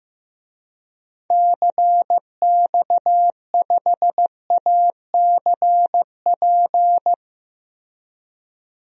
Morse code, heard as a keyed tone, about 16 wpm, 700 Hz. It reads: CX5ACP
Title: DXCC-100 Call Signs - 6 Characters - Repeated 2x - 15wpm